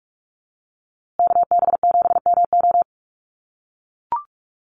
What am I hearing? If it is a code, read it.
Morse code, heard as a keyed tone, about 44 wpm, 700 Hz. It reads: KB7MO
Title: USA Call Signs Repeated 2x 45wpm